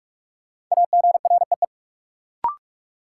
Morse code, heard as a keyed tone, about 46 wpm, 700 Hz. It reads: AGREE
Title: Sentences from Top 1100 Words - New Word Drill Alt - 45wpm